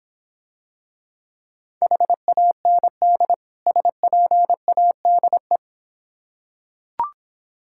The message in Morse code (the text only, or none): HAND SPADE